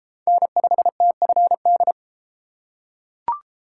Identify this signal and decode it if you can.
Morse code, heard as a keyed tone, about 31 wpm, 700 Hz. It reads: N5TFD